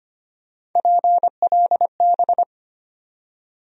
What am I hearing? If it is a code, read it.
Morse code, heard as a keyed tone, about 25 wpm, 700 Hz. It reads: PLB
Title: USA Call Sign Postfixes - 3 Letters 25wpm